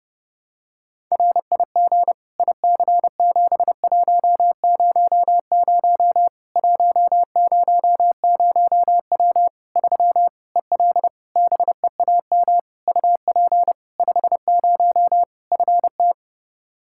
Morse code, heard as a keyed tone, about 30 wpm, 700 Hz. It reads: RIG IC7100 100W 3 EL BEAM UP 50 FT
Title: Practice QSO - Exchange 2 - Info - 30wpm